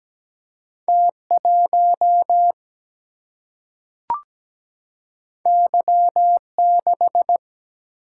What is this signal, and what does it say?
Morse code, heard as a keyed tone, about 17 wpm, 700 Hz. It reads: T 1 Y 6
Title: Single-Letter Single-Number 17wpm